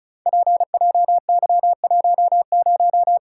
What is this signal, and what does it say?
Morse code, heard as a keyed tone, about 36 wpm, 700 Hz. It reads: PJY10
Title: Code Groups - 5 Letter:Numbers - 35wpm